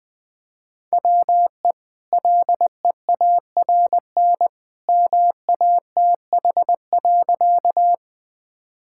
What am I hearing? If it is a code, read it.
Morse code, heard as a keyed tone, about 20 wpm, 700 Hz. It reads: WE LEARN MATH.
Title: Easy Sentences with Hints - 20wpm